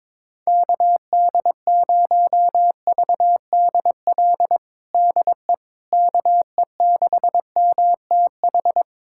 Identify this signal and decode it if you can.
Morse code, heard as a keyed tone, about 22 wpm, 700 Hz. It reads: KD0VDL DE KE6MTH
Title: USA Callsign DE Callsign - 6 Characters - 22wpm